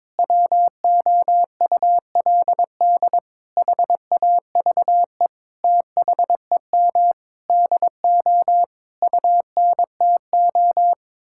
Morse code, heard as a keyed tone, about 22 wpm, 700 Hz. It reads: WOULD HAVE THEM DO UNTO
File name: practice / 22wpm / Sentences with 8-Word Phrases - 22wpm